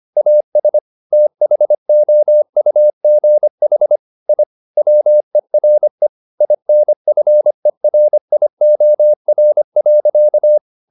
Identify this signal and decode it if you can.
Morse code, heard as a keyed tone, about 25 wpm, 600 Hz. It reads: AS THOUGH I WERE INFERIOR.
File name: the old man and the sea part 1 25wpm